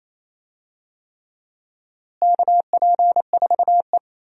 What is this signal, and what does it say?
Morse code, heard as a keyed tone, about 28 wpm, 700 Hz. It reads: KP4E
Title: DXCC-100 Call Signs - 4 Characters - Repeated 3x - 28wpm